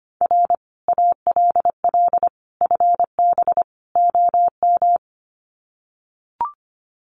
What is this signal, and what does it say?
Morse code, heard as a keyed tone, about 25 wpm, 700 Hz. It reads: R ALL FB OM
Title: Common QSO Phrases 25wpm